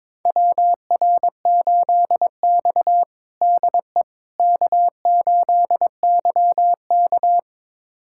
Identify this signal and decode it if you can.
Morse code, heard as a keyed tone, about 21 wpm, 700 Hz. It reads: WR8X DE K8YK
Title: USA Callsign DE Callsign - 4 Characters - 22wpm